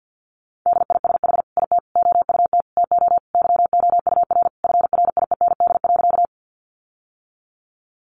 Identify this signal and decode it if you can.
Morse code, heard as a keyed tone, about 50 wpm, 700 Hz. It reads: DISH IT OUT TO YOUR FRIEND.
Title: Sentences from Top 1300 Words 50wpm